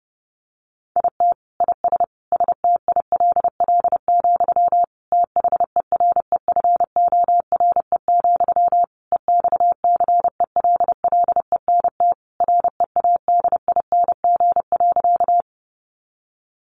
Morse code, heard as a keyed tone, about 30 wpm, 700 Hz. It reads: IT IS STILL, THEREFORE, EXCELLENT READING.
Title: Sentences from Top 1000 Words 30wpm